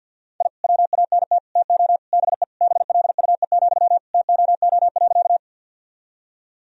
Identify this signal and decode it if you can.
Morse code, heard as a keyed tone, about 47 wpm, 700 Hz. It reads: I WANT TO BE BLUE, TOO.
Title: Sentences from Top 1000 Words - Review Part 3 - 50wpm